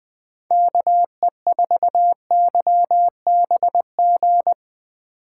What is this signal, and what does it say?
Morse code, heard as a keyed tone, about 20 wpm, 700 Hz. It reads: KE4YBG